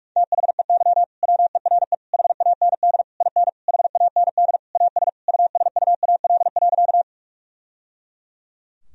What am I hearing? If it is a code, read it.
Morse code, heard as a keyed tone, about 46 wpm, 700 Hz. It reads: THEY WERE HAND IN HAND AS USUAL.
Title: Sentences from Top 1000 Words - Review Part 2 - 45wpm